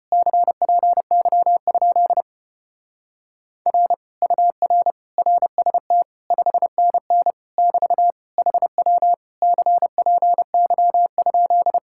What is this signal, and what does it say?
Morse code, heard as a keyed tone, about 33 wpm, 700 Hz. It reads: CPY? R UR RST 5NN <BT> HW CPY?